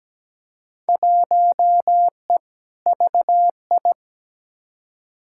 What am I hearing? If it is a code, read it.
Morse code, heard as a keyed tone, about 17 wpm, 700 Hz. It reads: 1E VI
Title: Field Day Exchanges - 17wpm